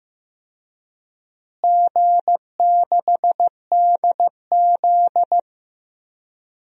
Morse code, heard as a keyed tone, about 15 wpm, 700 Hz. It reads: G6DZ
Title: DXCC-100 Call Signs - 4 Characters - Repeated 3x - 15wpm